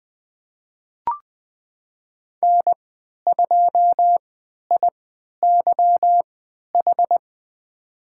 Morse code, heard as a keyed tone, about 20 wpm, spaced as for 11 wpm, 700 Hz. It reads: N2IYH
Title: USA Call Signs - 3x Character Spacing - 20wpm